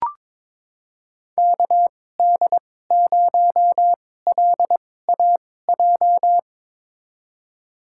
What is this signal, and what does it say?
Morse code, heard as a keyed tone, about 22 wpm, spaced as for 16 wpm, 700 Hz. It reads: KD0LAJ
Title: USA Call Signs - 6 Characters - 2x Character Spacing - 22wpm